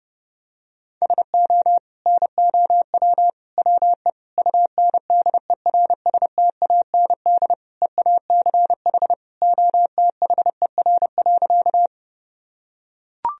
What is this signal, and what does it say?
Morse code, heard as a keyed tone, about 32 wpm, 700 Hz. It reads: SO NOW WE UNDERSTAND EACH OTHER.
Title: Sentences from Top 1000 Words - Review Part 1 - 30wpm